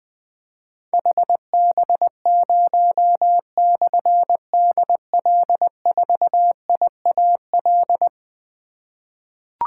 Morse code, heard as a keyed tone, about 20 wpm, 700 Hz. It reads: HB0/DL4IAL
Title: Difficult Call Signs 20wpm